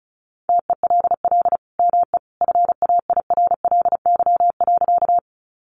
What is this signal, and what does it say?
Morse code, heard as a keyed tone, about 34 wpm, 700 Hz. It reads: TELL ME FAIRLY.
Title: Sentences from Top 1100 Words 35wpm